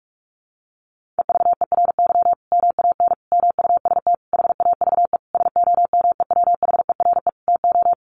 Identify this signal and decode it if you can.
Morse code, heard as a keyed tone, about 44 wpm, 700 Hz. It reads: EVERY MAN MUST HAVE SOMEWHERE TO
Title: Sentences from Top 1400 Words 45wpm